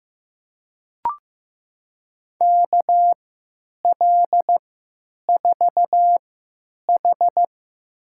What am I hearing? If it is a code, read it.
Morse code, heard as a keyed tone, about 15 wpm, spaced as for 9 wpm, 700 Hz. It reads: KL4H
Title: USA Call Signs - 4 Characters - 3x Character Spacing - 15wpm